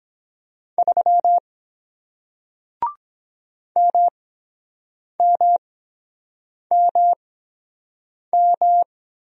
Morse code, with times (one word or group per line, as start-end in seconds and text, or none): 0.8-1.4 3
3.8-8.8 MMMM